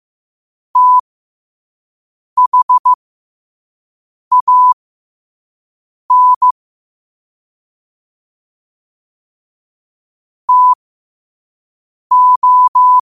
Morse code, heard as a keyed tone, about 15 wpm, spaced as for 5 wpm, 1000 Hz. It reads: THAN TO
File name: qotdfiles / 5wpm